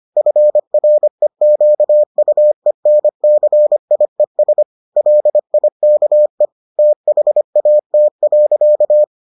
Morse code, heard as a keyed tone, about 25 wpm, 600 Hz. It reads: FREQUENCIES LIKE THAT.